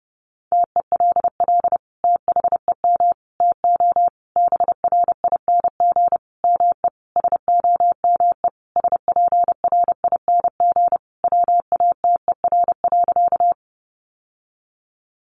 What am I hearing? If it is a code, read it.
Morse code, heard as a keyed tone, about 30 wpm, 700 Hz. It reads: TELL THEM TO BRING ME SOME SPRING WATER.
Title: Sentences from Top 1200 Words 30wpm